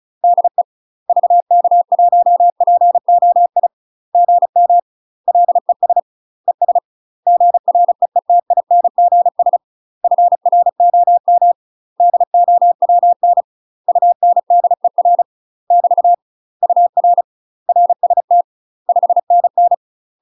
Morse code, heard as a keyed tone, about 34 wpm, 700 Hz. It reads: DE VK1POI GM LES ES GREETINGS FROM DOWN UNDER <BT> UR RST 5NN
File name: UK MASTER QSOs TRAINING / 35WPM / UK MASTER QSOs 0003